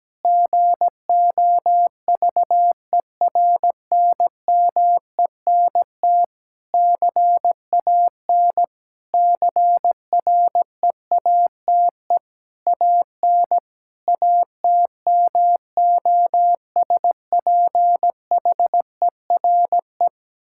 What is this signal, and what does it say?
Morse code, heard as a keyed tone, about 17 wpm, 700 Hz. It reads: GOVERNMENT CAN CREATE AN ATMOSPHERE
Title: Sentences from Top 1600 Words 17wpm